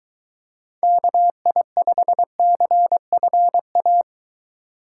Morse code, heard as a keyed tone, about 24 wpm, 700 Hz. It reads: KI5CFA